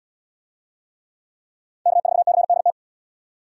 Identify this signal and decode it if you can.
Morse code, heard as a keyed tone, about 68 wpm, 700 Hz. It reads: K4CDT